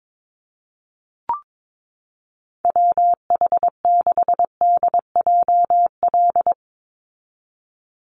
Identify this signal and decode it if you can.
Morse code, heard as a keyed tone, about 22 wpm, 700 Hz. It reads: WH6DJL
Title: USA Call Signs 22wpm